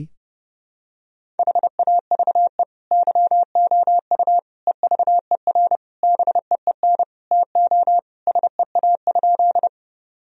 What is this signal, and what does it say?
Morse code, heard as a keyed tone, about 32 wpm, 700 Hz. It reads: HAVE YOU EVER BEEN TO SEA?
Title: Sentences from Top 1000 Words - Review Part 1 - 30wpm